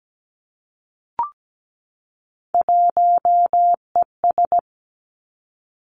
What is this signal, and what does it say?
Morse code, heard as a keyed tone, about 17 wpm, 700 Hz. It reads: 1ES
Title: Three Challenging Letter-Number Combinations 17wpm